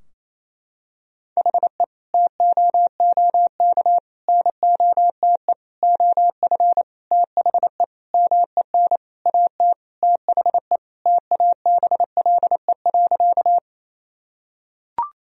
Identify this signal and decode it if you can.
Morse code, heard as a keyed tone, about 28 wpm, 700 Hz. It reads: HE TOOK NOTE OF THE MEN AT THE TABLE.
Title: Sentences from Top 1000 Words - Review Part 2 - 28wpm